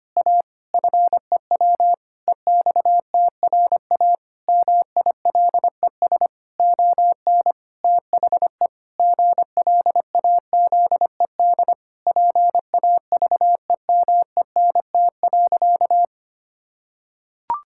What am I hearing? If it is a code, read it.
Morse code, heard as a keyed tone, about 25 wpm, 700 Hz. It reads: A FEW EXTRA MILES ON THE GLAZED PAVEMENT.
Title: Pangram Challenge v2 - 25wpm